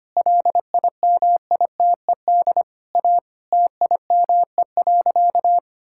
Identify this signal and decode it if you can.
Morse code, heard as a keyed tone, about 25 wpm, 700 Hz. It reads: LIMITED A TIME.
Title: Sentences from Top 1000 Words - Review Part 6 - 25wpm